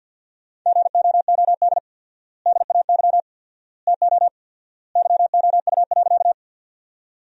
Morse code, heard as a keyed tone, about 53 wpm, 700 Hz. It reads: GOOD DAY TO YOU.
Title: 4x Word-Spacing - Sentences from Top 200 Words - 50wpm